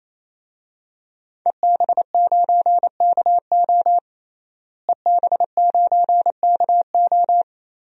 Morse code, heard as a keyed tone, about 27 wpm, 700 Hz. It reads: EB9KO EB9KO
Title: DXCC-100 Call Signs - 5 Characters - Repeated 3x - 28wpm